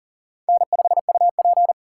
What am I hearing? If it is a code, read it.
Morse code, heard as a keyed tone, about 40 wpm, 700 Hz. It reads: N5UP
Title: USA Call Signs - 2x Character Spacing - 40wpm